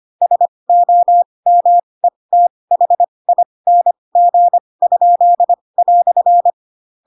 Morse code, heard as a keyed tone, about 25 wpm, spaced as for 20 wpm, 700 Hz. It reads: SOMETHING?"
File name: Pinocchio / Pinocchio ch4 15 25 ik1hge